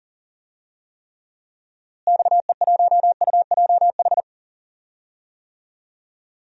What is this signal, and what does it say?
Morse code, heard as a keyed tone, about 37 wpm, 700 Hz. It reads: XE1UJH
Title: DXCC-100 Call Signs - 6 Characters - Repeated 3x - 40wpm